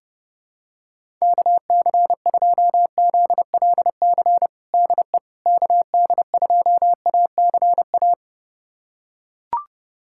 Morse code, heard as a keyed tone, about 30 wpm, 700 Hz. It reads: KC2ZLC DE KD2ACA
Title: USA Callsign DE Callsign 30wpm